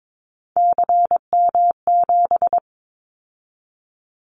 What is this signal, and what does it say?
Morse code, heard as a keyed tone, about 22 wpm, 700 Hz. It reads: CM7
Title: Callsign Prefixes for Top 100 DXCC - 22wpm